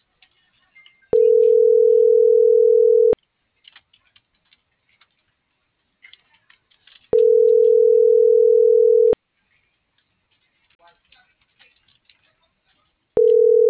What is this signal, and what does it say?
Ringback tone (North America), 440 + 480 Hz.